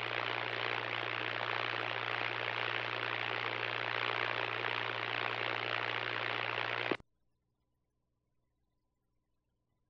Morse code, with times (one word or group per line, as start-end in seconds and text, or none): none